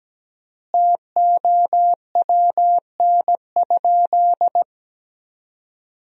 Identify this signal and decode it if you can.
Morse code, heard as a keyed tone, about 17 wpm, 700 Hz. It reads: TOWN?
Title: 3x Word-Spacing - Sentences from Top 500 Words - 17wpm